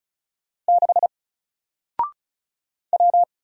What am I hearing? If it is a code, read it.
Morse code, heard as a keyed tone, about 36 wpm, 700 Hz. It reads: 6 W
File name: Single-Letter-Number-35wpm